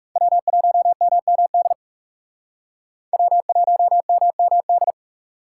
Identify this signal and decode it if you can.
Morse code, heard as a keyed tone, about 36 wpm, 700 Hz. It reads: W1MMD W1MMD
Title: Speed Racing - USA 1x3 Call Signs 30wpm